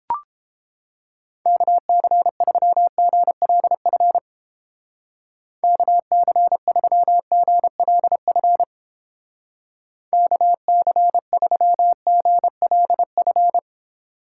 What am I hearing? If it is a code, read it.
Morse code, heard as a keyed tone, about 31 wpm, 700 Hz. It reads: KC3GLF KC3GLF KC3GLF